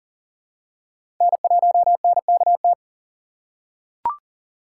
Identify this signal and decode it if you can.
Morse code, heard as a keyed tone, about 40 wpm, 700 Hz. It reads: N1NKT
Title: USA Call Signs - 3x Character Spacing - 40wpm